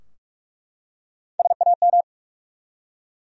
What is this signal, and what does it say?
Morse code, heard as a keyed tone, about 44 wpm, 700 Hz. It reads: SAM
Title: USA Common Names - Max 3 Letters - 45wpm